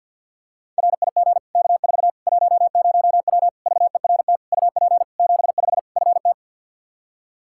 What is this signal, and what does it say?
Morse code, heard as a keyed tone, about 51 wpm, 700 Hz. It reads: RIG K4 10W VERT UP 75 FT
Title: Practice QSO - Exchange 2 - Info - 50wpm